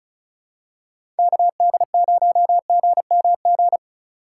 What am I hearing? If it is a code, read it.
Morse code, heard as a keyed tone, about 36 wpm, 700 Hz. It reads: KD0GMG
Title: USA Call Signs - 6 Characters - 4x Character Spacing - 35wpm